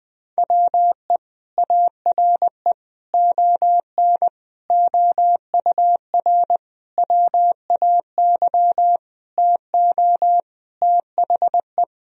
Morse code, heard as a keyed tone, about 21 wpm, 700 Hz. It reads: WE ARE ON OUR WAY TO THE
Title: Sentences from Top 1000 Words - Review Part 2 - 20wpm